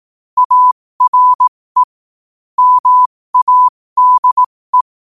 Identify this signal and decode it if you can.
Morse code, heard as a keyed tone, about 18 wpm, spaced as for 15 wpm, 1000 Hz. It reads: ARE MADE
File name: qotdfiles / 15wpm